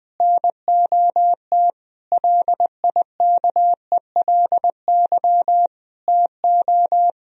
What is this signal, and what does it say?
Morse code, heard as a keyed tone, about 20 wpm, 700 Hz. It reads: NOT LIKELY TO
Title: Sentences from Top 1000 Words - Review Part 4 - 20wpm